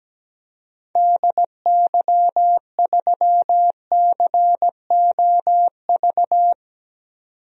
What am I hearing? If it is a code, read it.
Morse code, heard as a keyed tone, about 17 wpm, 700 Hz. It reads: DY3COV
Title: DXCC-100 Call Signs - 6 Characters - 17wpm